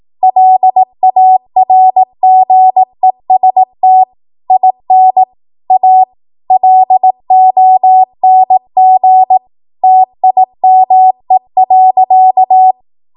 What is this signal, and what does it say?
Morse code, heard as a keyed tone, about 18 wpm, 750 Hz. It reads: LARGEST IN A LONG TIME.
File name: Archive / 18 WPM / 18WPM